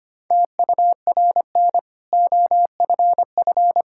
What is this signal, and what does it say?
Morse code, heard as a keyed tone, about 25 wpm, 700 Hz. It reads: TURN OFF
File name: Easy Sentences with Hints - 25wpm